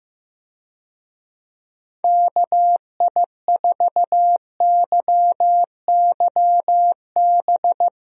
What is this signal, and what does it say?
Morse code, heard as a keyed tone, about 15 wpm, 700 Hz. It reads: KI4YYB
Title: USA Call Signs - 6 Characters - Repeated 3x - 15wpm